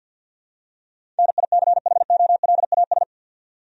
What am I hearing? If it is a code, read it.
Morse code, heard as a keyed tone, about 47 wpm, 700 Hz. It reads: NICHOLAS
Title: USA Common Names - 50wpm